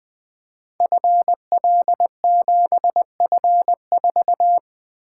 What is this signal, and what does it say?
Morse code, heard as a keyed tone, about 20 wpm, 700 Hz. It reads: FL7F4
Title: Code Groups - 5 Letter:Numbers - 20wpm